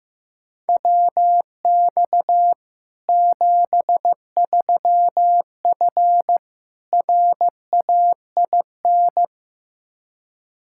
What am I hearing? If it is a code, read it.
Morse code, heard as a keyed tone, about 16 wpm, 700 Hz. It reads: WX 73F RAIN
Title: Practice QSO - Exchange 2 - Info - 15wpm